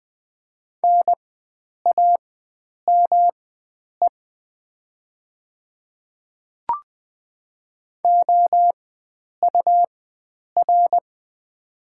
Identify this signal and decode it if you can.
Morse code, heard as a keyed tone, about 20 wpm, spaced as for 9 wpm, 700 Hz. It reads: NAME OUR
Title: 4x Character Spacing - Top 500 Words - 20wpm